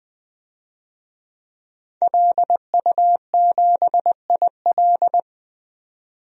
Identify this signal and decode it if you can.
Morse code, heard as a keyed tone, about 20 wpm, 700 Hz. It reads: LU7IL
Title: DXCC-100 Call Signs - 5 Characters - 20wpm